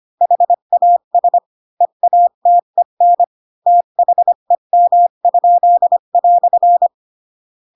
Morse code, heard as a keyed tone, about 25 wpm, 700 Hz. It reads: HAS EATEN THEM?"
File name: Pinocchio / Pinocchio ch7 20 25 ik1hge